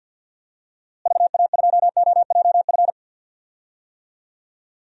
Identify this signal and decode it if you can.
Morse code, heard as a keyed tone, about 48 wpm, 700 Hz. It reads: VA2OJF
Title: DXCC-100 Call Signs - 6 Characters - Repeated 3x - 50wpm